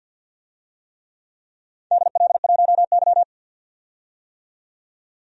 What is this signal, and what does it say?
Morse code, heard as a keyed tone, about 48 wpm, 700 Hz. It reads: DL1Y
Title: DXCC-100 Call Signs - 4 Characters - 50wpm